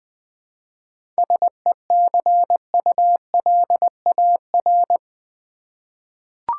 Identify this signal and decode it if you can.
Morse code, heard as a keyed tone, about 20 wpm, 700 Hz. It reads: SECULAR
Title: 7 Letter Words - First 3 Letters Given - 20wpm